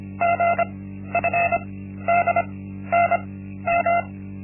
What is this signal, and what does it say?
Morse code, heard as a keyed tone, about 25 wpm, spaced as for 13 wpm, 690 Hz. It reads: GFDNM
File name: Linksmakalnis-1